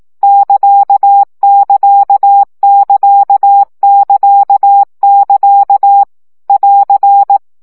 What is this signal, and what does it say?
Morse code, heard as a keyed tone, about 18 wpm, 800 Hz. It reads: <KA><KA><KA><KA><KA> <AR>
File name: CW gyakorlas ct 15 baud